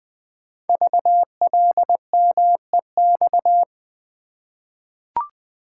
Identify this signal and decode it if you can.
Morse code, heard as a keyed tone, about 20 wpm, 700 Hz. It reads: VLMEX
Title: Code Groups - 5 Letters - 20wpm